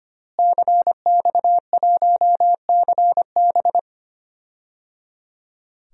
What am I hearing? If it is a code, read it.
Morse code, heard as a keyed tone, about 25 wpm, 700 Hz. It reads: CX1CB
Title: DXCC-100 Call Signs - 5 Characters - Repeated 3x - 25wpm